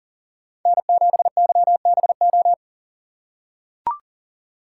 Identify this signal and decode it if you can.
Morse code, heard as a keyed tone, about 40 wpm, 700 Hz. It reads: N7YBO
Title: USA Call Signs - 5 Characters - 4x Character Spacing - 40wpm